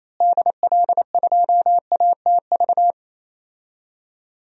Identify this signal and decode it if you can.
Morse code, heard as a keyed tone, about 28 wpm, 700 Hz. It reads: DL2ATV
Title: DXCC-100 Call Signs - 6 Characters - Repeated 3x - 28wpm